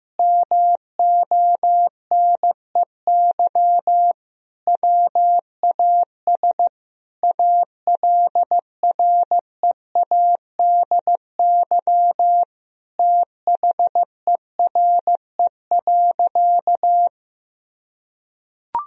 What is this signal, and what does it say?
Morse code, heard as a keyed tone, about 16 wpm, 700 Hz. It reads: MONEY WAS ALREADY THERE.
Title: Sentences from Top 1000 Words - Review Part 1 - 15wpm